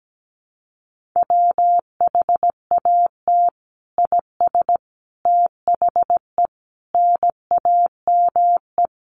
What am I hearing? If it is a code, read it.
Morse code, heard as a keyed tone, about 17 wpm, 700 Hz. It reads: WHAT IS THE NAME
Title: Sentences from Top 1500 Words 17wpm